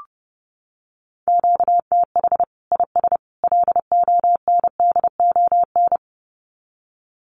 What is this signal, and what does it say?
Morse code, heard as a keyed tone, about 30 wpm, 700 Hz. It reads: QTH IS LONDON
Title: Common QSO Phrases 30wpm